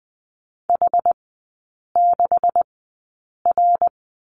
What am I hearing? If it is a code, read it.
Morse code, heard as a keyed tone, about 20 wpm, spaced as for 8 wpm, 700 Hz. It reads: H6R
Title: 152 Lesson 25e - Letter p - Three Characters 20wpm